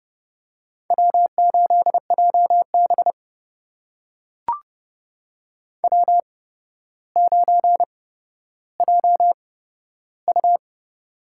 Morse code, heard as a keyed tone, about 32 wpm, spaced as for 13 wpm, 700 Hz. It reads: W8JB W9JU
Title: USA Call Signs - 4 Characters - 8x Character Spacing - 30wpm